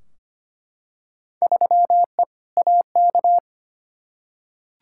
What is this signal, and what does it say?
Morse code, heard as a keyed tone, about 25 wpm, 700 Hz. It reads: 3E AK
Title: Field Day Exchanges Encore - 25wpm